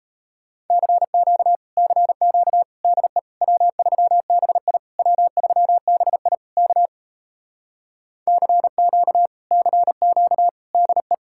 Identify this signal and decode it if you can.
Morse code, heard as a keyed tone, about 37 wpm, 700 Hz. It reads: CQ CQ DE W3BI W3BI K CQ CQ DE